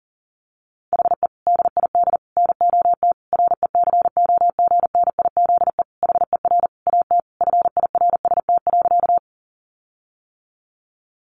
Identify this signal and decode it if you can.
Morse code, heard as a keyed tone, about 40 wpm, 700 Hz. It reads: HE DID NOT RECOGNIZE HER AT FIRST.
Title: Sentences from Top 1400 Words 40wpm